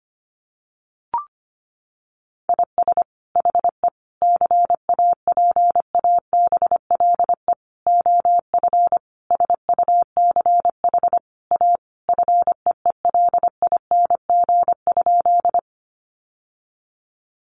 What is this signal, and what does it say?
Morse code, heard as a keyed tone, about 25 wpm, 700 Hz. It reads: IS HE CAPABLE OF SUCH A FEELING?
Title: Sentences from Top 1200 Words 25wpm